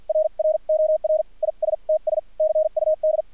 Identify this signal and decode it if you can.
Morse code, heard as a keyed tone, about 50 wpm, spaced as for 41 wpm, 620 Hz, with radio noise. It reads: RROR ISTS CUD
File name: archived / CRSnet